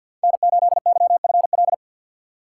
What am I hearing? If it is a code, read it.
Morse code, heard as a keyed tone, about 52 wpm, 700 Hz. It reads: N8YFL
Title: USA Call Signs - 2x Character Spacing - 50wpm